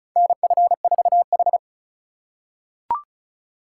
Morse code, heard as a keyed tone, about 34 wpm, 690 Hz. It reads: NF4H
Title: USA Call Signs - 3x Character Spacing - 35wpm